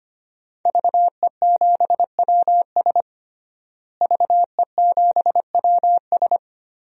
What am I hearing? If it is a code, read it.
Morse code, heard as a keyed tone, about 25 wpm, 700 Hz. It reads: VE7WS VE7WS
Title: DXCC-100 Call Signs - 5 Characters - Repeated 2x - 25wpm